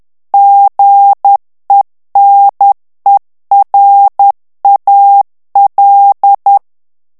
Morse code, heard as a keyed tone, about 10 wpm, 800 Hz. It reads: GENERAL
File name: ANGLAIS texte clair 3 vitesse 10 mots minute